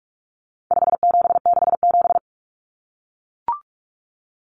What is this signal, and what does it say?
Morse code, heard as a keyed tone, about 44 wpm, 700 Hz. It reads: 5767
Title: Common Cut Numbers - 4-Digit 45wpm